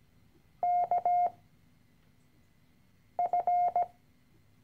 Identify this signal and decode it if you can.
Morse code, heard as a keyed tone, about 17 wpm, spaced as for 4 wpm, 710 Hz. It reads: KF
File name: CW-Online-Kurs / Anfänger / Lektion 013